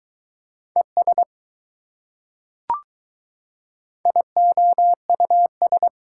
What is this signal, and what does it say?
Morse code, heard as a keyed tone, about 20 wpm, 700 Hz. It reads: ES IOUS